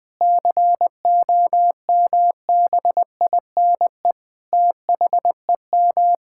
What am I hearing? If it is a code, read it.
Morse code, heard as a keyed tone, about 20 wpm, 700 Hz. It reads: COMBINE THEM
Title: Sentences from Top 1600 Words 20wpm